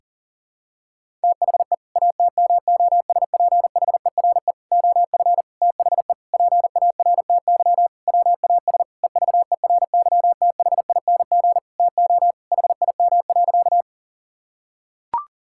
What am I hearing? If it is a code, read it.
Morse code, heard as a keyed tone, about 40 wpm, 700 Hz. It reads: THE ATMOSPHERE OF THE PARTY WAS EVERYTHING TO HIM.
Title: Sentences from Top 1600 Words 40wpm